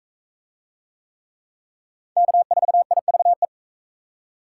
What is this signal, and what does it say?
Morse code, heard as a keyed tone, about 41 wpm, 700 Hz. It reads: K4IVE